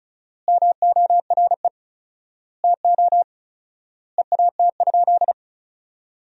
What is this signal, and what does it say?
Morse code, heard as a keyed tone, about 34 wpm, 700 Hz. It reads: MORE TO EAT?
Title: 4x Word-Spacing - Sentences from Top 300 Words - 35wpm